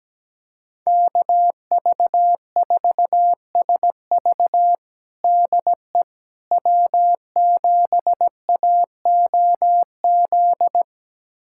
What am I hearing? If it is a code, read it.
Morse code, heard as a keyed tone, about 17 wpm, 700 Hz. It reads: KV4SV DE W7AOZ
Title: USA Callsign DE Callsign - 5 Characters - 17wpm